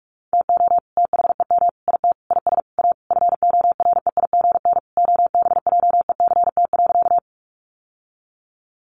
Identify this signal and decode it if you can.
Morse code, heard as a keyed tone, about 44 wpm, 700 Hz. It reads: TO THEM IT IS A FOREIGN OBJECT.
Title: Sentences from Top 1400 Words 45wpm